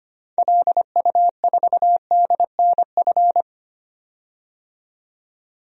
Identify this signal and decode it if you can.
Morse code, heard as a keyed tone, about 25 wpm, 700 Hz. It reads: LU4DNF
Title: DXCC-100 Call Signs - 6 Characters - 25wpm